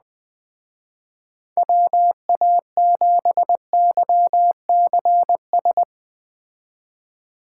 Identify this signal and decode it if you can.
Morse code, heard as a keyed tone, about 20 wpm, 700 Hz. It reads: WA7YCS
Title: Speed Racing - USA 2x3 Call Signs 17wpm